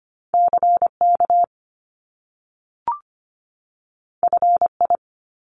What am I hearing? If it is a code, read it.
Morse code, heard as a keyed tone, about 25 wpm, 700 Hz. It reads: CK FI
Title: Top Two Letter Combinations in Words 25wpm